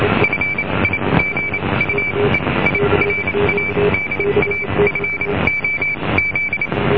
Morse code, heard as a keyed tone, about 28 wpm, spaced as for 22 wpm, 2300 Hz, with noise. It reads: FIGUEIRA DA FOZ